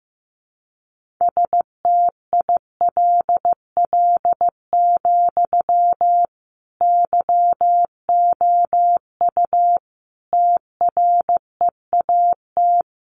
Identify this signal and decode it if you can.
Morse code, heard as a keyed tone, about 15 wpm, 700 Hz. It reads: STILL, YOU TREAT
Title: Sentences from Top 1300 Words 15wpm